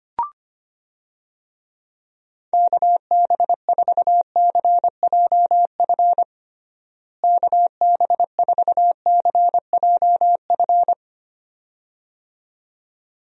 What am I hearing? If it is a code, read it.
Morse code, heard as a keyed tone, about 25 wpm, 700 Hz. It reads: KB4CJF KB4CJF
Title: USA Call Signs - 6 Characters - Repeated 2x - 25wpm